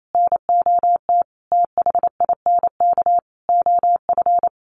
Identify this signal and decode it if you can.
Morse code, heard as a keyed tone, about 28 wpm, 700 Hz. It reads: NOT THINK OF
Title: Sentences from Top 1000 Words 28wpm